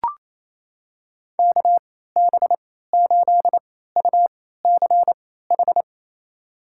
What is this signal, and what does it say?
Morse code, heard as a keyed tone, about 28 wpm, spaced as for 16 wpm, 700 Hz. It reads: KB8UCH
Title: USA Call Signs - 6 Characters - 3x Character Spacing - 28wpm